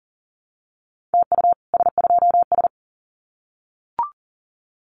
Morse code, heard as a keyed tone, about 38 wpm, 700 Hz. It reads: TU S2S
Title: Common QSO Phrases Encore 40wpm